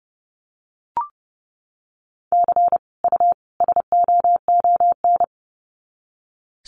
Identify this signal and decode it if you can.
Morse code, heard as a keyed tone, about 30 wpm, 700 Hz. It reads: C U SOON
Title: Common QSO Phrases 30wpm